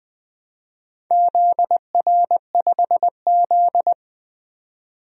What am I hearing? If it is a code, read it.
Morse code, heard as a keyed tone, about 20 wpm, 700 Hz. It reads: ZR5Z
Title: DXCC-100 Call Signs - 4 Characters - 20wpm